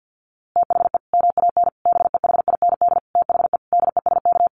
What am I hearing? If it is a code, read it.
Morse code, heard as a keyed tone, about 50 wpm, 700 Hz. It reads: THE MAN BEHIND THE DESK
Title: A Sound Of Thunder 50wpm